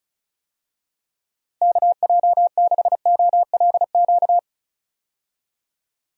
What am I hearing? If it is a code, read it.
Morse code, heard as a keyed tone, about 36 wpm, 700 Hz. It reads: KJ6OLQ
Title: USA Call Signs - 6 Characters - Repeated 2x - 35wpm